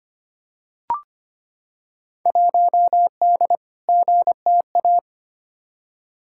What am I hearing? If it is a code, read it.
Morse code, heard as a keyed tone, about 25 wpm, 700 Hz. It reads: 1D GTA
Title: Field Day Exchanges - 25wpm